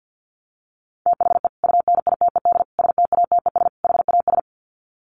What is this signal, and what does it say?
Morse code, heard as a keyed tone, about 50 wpm, 700 Hz. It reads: THE UNITED STATES HAS
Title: Sets of 4 Words 50wpm